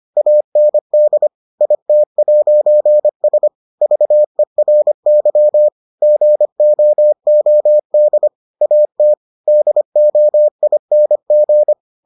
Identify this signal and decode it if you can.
Morse code, heard as a keyed tone, about 25 wpm, 600 Hz. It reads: AND IT'S VERY GOOD AT DOING